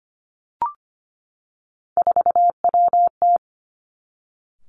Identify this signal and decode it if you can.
Morse code, heard as a keyed tone, about 25 wpm, 700 Hz. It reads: 4WT
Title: Three Challenging Letter-Number Combinations 25wpm